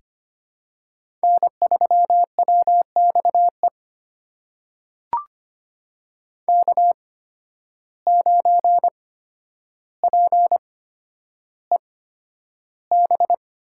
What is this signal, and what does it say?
Morse code, heard as a keyed tone, about 25 wpm, spaced as for 11 wpm, 700 Hz. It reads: N3WXE K9PEB
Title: USA Call Signs - 5 Characters - 8x Character Spacing - 25wpm